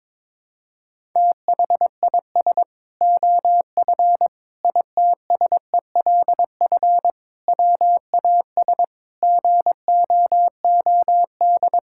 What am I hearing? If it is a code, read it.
Morse code, heard as a keyed tone, about 22 wpm, 700 Hz. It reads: THIS OF ITSELF WAS GOOD
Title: Sentences from Top 1000 Words - Review Part 5 - 22wpm